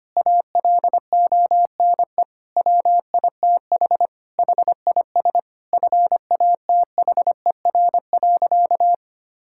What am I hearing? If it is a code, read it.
Morse code, heard as a keyed tone, about 25 wpm, 700 Hz. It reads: ALONE WITH HIS FATHER.
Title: Sentences from Top 1000 Words - Review Part 5 - 25wpm